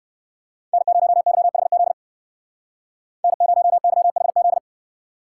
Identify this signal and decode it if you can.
Morse code, heard as a keyed tone, about 54 wpm, 700 Hz. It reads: N0YHZ N0YHZ
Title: Speed Racing - USA 1x3 Call Signs 45wpm